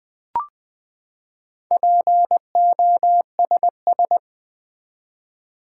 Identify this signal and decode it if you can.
Morse code, heard as a keyed tone, about 20 wpm, 700 Hz. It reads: POSS
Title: Build A Word - 4 Letters At A Time - 10-Letter Words - 20wpm